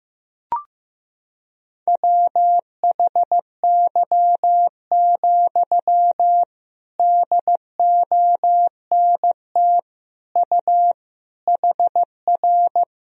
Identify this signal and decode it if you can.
Morse code, heard as a keyed tone, about 16 wpm, 700 Hz. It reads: WHY, DONT U HR
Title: Sentences from Top 300 Words and Abbreviations 15wpm